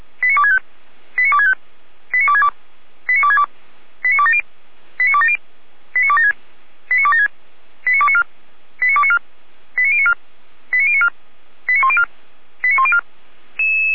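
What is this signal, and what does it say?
Five-tone selective call (ZVEI), heard as quick runs of five tones, tones of about 70 ms. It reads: 89265 89265 89262 89262 89270 89270 89276 89276 89294 89294 89094 89094 89104 89104
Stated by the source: ZVEI five-tone selective call received on SDR.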